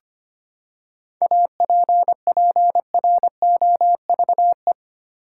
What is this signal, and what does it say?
Morse code, heard as a keyed tone, about 25 wpm, 700 Hz. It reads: APPROVE
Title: Seven Letter Words - 25wpm